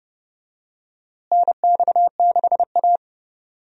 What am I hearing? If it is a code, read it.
Morse code, heard as a keyed tone, about 30 wpm, 700 Hz. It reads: NX6A
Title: USA Call Signs - 4 Characters - Repeated 3x - 30wpm